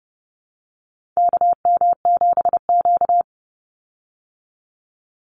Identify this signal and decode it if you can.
Morse code, heard as a keyed tone, about 30 wpm, 700 Hz. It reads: KM7Q
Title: Call Signs Repeated 2x - 30wpm